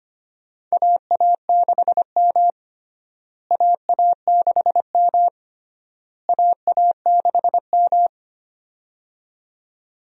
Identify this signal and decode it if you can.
Morse code, heard as a keyed tone, about 25 wpm, 700 Hz. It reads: AA6M AA6M AA6M
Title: USA Call Signs - 4 Characters - Repeated 3x - 25wpm